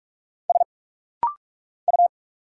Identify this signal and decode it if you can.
Morse code, heard as a keyed tone, about 44 wpm, 700 Hz. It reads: S U
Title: Single Letter-Number Alt - Rapid-Fire 45wpm